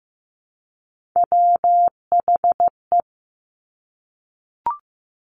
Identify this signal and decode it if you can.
Morse code, heard as a keyed tone, about 15 wpm, 700 Hz. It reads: WHE
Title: Top Three Letter Combinations in Words 15wpm